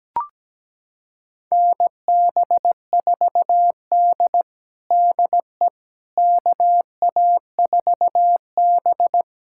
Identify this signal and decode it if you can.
Morse code, heard as a keyed tone, about 17 wpm, 700 Hz. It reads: NB4D DE KA4B
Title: USA Callsign DE Callsign - 4 Characters - 17wpm